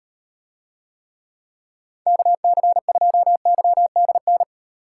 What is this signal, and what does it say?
Morse code, heard as a keyed tone, about 38 wpm, 700 Hz. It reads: KC2YDN